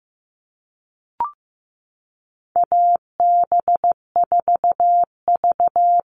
Morse code, heard as a keyed tone, about 15 wpm, 700 Hz. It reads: AB4V
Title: All Cut Numbers - 4-Digit 15wpm